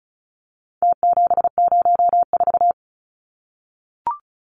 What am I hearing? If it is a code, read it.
Morse code, heard as a keyed tone, about 34 wpm, 700 Hz. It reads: T704
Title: Numbers - Contest Style - 4-Digit 35wpm